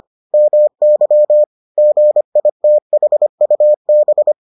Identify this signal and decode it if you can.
Morse code, heard as a keyed tone, about 25 wpm, 600 Hz. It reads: MY GITHUB